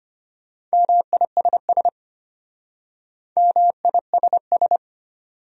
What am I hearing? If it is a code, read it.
Morse code, heard as a keyed tone, about 33 wpm, 700 Hz. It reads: MISS MISS